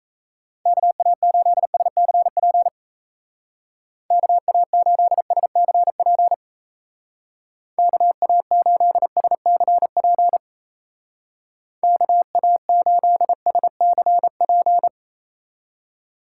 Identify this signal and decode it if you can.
Morse code, heard as a keyed tone, about 37 wpm, 700 Hz. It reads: KA8SCP KA8SCP KA8SCP KA8SCP